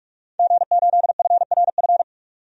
Morse code, heard as a keyed tone, about 46 wpm, 700 Hz. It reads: G8FRF
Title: DXCC-100 Call Signs - 5 Characters - 45wpm